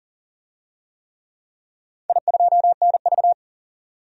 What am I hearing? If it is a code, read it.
Morse code, heard as a keyed tone, about 40 wpm, 700 Hz. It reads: I2NV
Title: DXCC-100 Call Signs - 4 Characters - 40wpm